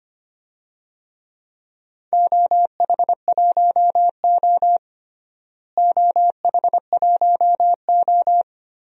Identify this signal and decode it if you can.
Morse code, heard as a keyed tone, about 25 wpm, 700 Hz. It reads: OH1O OH1O
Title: DXCC-100 Call Signs - 4 Characters - Repeated 3x - 25wpm